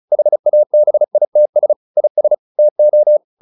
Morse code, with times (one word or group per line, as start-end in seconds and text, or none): 0.1-1.7 HABITS
2.0-2.3 IS
2.6-3.2 TO